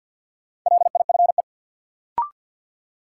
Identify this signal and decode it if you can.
Morse code, heard as a keyed tone, about 51 wpm, 700 Hz. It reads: LIFE
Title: Sentences from Top 300 Words - New Word Drill - 50wpm